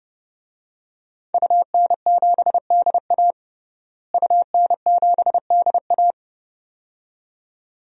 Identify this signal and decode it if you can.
Morse code, heard as a keyed tone, about 30 wpm, 700 Hz. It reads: UN7DA UN7DA
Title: DXCC-100 Call Signs - 5 Characters - Repeated 2x - 30wpm